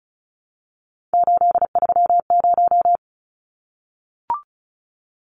Morse code, heard as a keyed tone, about 34 wpm, 700 Hz. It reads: 830
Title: Numbers - 3-Digit 35wpm